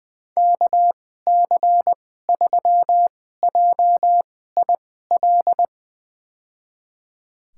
Morse code, heard as a keyed tone, about 20 wpm, spaced as for 15 wpm, 700 Hz. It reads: KC3JIL
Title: USA Call Signs - 6 Characters - 2x Character Spacing - 20wpm